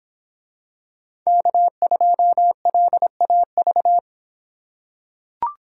Morse code, heard as a keyed tone, about 26 wpm, 700 Hz. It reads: K2LAV